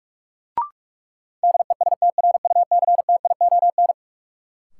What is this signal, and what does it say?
Morse code, heard as a keyed tone, about 46 wpm, 700 Hz. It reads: DESTRUCTION
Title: Sets of 5 Words Encore - New Word Drill - 45wpm